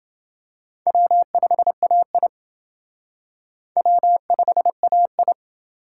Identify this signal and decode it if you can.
Morse code, heard as a keyed tone, about 28 wpm, 700 Hz. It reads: W5AI W5AI